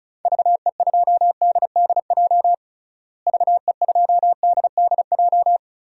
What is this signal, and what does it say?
Morse code, heard as a keyed tone, about 36 wpm, 700 Hz. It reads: VE2DDJ VE2DDJ
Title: DXCC-100 Call Signs - 6 Characters - Repeated 3x - 35wpm